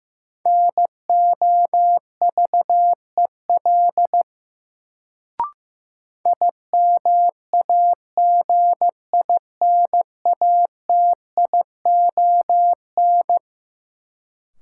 Morse code, heard as a keyed tone, about 15 wpm, 700 Hz. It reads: NOVEL IMAGINATION
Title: Sentences from Top 1300 Words - New Word Drill Alt - 15wpm